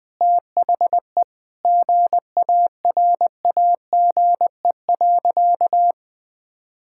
Morse code, heard as a keyed tone, about 21 wpm, 700 Hz. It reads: THE GARAGE.
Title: Sentences from Top 1600 Words 20wpm